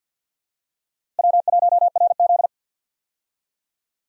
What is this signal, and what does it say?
Morse code, heard as a keyed tone, about 53 wpm, 700 Hz. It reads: W1RZ
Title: USA Call Signs - 4 Characters - Repeated 2x - 50wpm